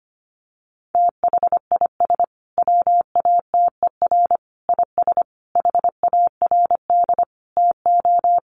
Morse code, heard as a keyed tone, about 25 wpm, 700 Hz. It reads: THIS WATER IS HARD TO
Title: Sentences from Top 1300 Words 25wpm